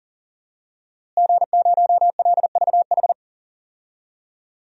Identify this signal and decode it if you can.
Morse code, heard as a keyed tone, about 40 wpm, 700 Hz. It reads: G0LVH
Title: DXCC-100 Call Signs - 5 Characters - Repeated 2x - 40wpm